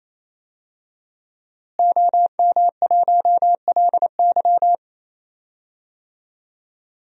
Morse code, heard as a keyed tone, about 28 wpm, 700 Hz. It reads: OM1LY
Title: DXCC-100 Call Signs - 5 Characters - Repeated 3x - 28wpm